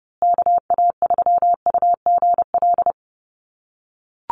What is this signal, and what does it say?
Morse code, heard as a keyed tone, about 30 wpm, 700 Hz. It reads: KA3UGL
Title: USA Call Signs Repeated 2x 30wpm